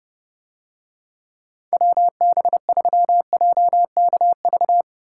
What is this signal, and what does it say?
Morse code, heard as a keyed tone, about 30 wpm, 700 Hz. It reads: WB3JKV